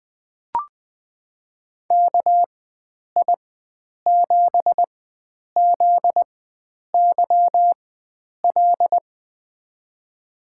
Morse code, heard as a keyed tone, about 20 wpm, spaced as for 9 wpm, 700 Hz. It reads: KI7ZYL
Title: USA Call Signs - 4x Character Spacing - 20wpm